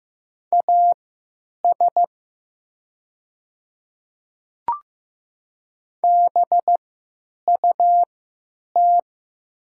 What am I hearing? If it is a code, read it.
Morse code, heard as a keyed tone, about 15 wpm, spaced as for 9 wpm, 700 Hz. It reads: AS BUT
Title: 3x Character Spacing - Top 100 Words - 15wpm